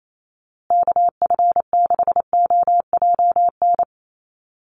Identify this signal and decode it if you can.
Morse code, heard as a keyed tone, about 28 wpm, 700 Hz. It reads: KF6OJN
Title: USA Call Signs Repeated 2x 28wpm